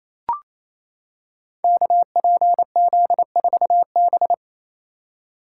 Morse code, heard as a keyed tone, about 29 wpm, 690 Hz. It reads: KPZ4B
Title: Code Groups - 5 Letter:Numbers - 28wpm